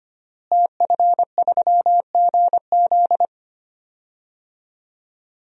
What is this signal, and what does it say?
Morse code, heard as a keyed tone, about 25 wpm, 700 Hz. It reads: TF3GZ
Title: DXCC-100 Call Signs - 5 Characters - Repeated 3x - 25wpm